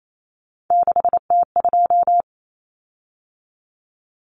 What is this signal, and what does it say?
Morse code, heard as a keyed tone, about 28 wpm, 700 Hz. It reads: 6T2
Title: Common Cut Numbers - 3-Digit 28wpm